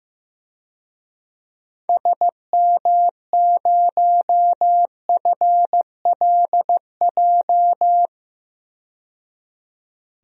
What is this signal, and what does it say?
Morse code, heard as a keyed tone, about 15 wpm, 700 Hz. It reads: SM0FLJ
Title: DXCC-100 Call Signs - 6 Characters - 15wpm